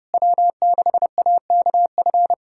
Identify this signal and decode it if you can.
Morse code, heard as a keyed tone, about 30 wpm, 700 Hz. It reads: W6AKF
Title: USA Callsign DE Callsign - 5 Characters - 30wpm